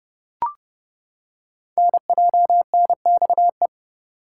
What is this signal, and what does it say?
Morse code, heard as a keyed tone, about 28 wpm, 700 Hz. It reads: NJNXE
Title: Code Groups - 5 Letters - 30wpm